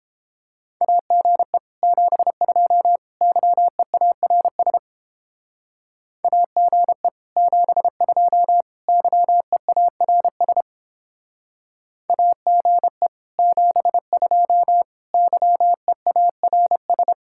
Morse code, heard as a keyed tone, about 29 wpm, 700 Hz. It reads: AGE 72 YEARS AGE 72 YEARS AGE 72 YEARS